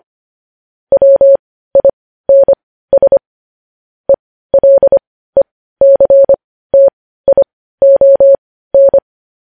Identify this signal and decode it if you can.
Morse code, heard as a keyed tone, about 25 wpm, spaced as for 15 wpm, 550 Hz. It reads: WINS ELECTION
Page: news-headlines-in-morse-code-at-15-wpm